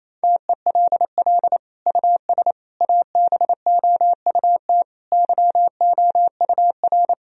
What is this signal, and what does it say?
Morse code, heard as a keyed tone, about 27 wpm, 700 Hz. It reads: TELL US ABOUT YOUR
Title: Sentences from Top 1000 Words - Review Part 4 - 28wpm